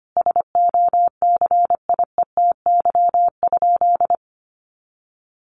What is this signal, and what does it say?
Morse code, heard as a keyed tone, about 25 wpm, 700 Hz. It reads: SOCIETY?
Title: Sentences from Top 700 Words 25wpm